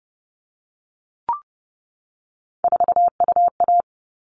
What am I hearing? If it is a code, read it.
Morse code, heard as a keyed tone, about 30 wpm, 700 Hz. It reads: 4UA
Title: All Cut Numbers - 3-Digit 30wpm